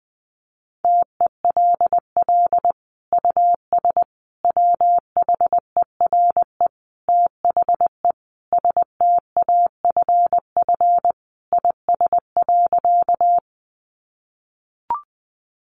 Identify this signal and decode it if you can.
Morse code, heard as a keyed tone, about 20 wpm, 700 Hz. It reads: TELL US WHERE THE STAFF IS.
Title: Sentences from Top 1200 Words 20wpm